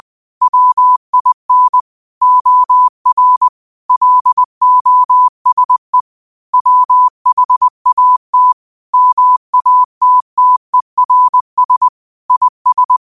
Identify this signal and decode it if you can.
Morse code, heard as a keyed tone, about 20 wpm, 1000 Hz. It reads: WIN OR LOSE WHAT MATTERS IS